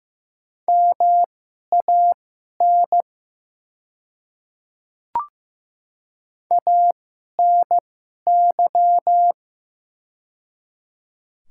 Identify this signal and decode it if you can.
Morse code, heard as a keyed tone, about 15 wpm, spaced as for 11 wpm, 700 Hz. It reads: MAN ANY
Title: 2x Character Spacing - Top 500 Words - 15wpm